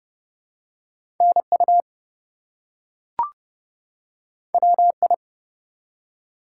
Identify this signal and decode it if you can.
Morse code, heard as a keyed tone, about 29 wpm, 700 Hz. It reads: NU WI
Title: US States and Canadian Provinces 30wpm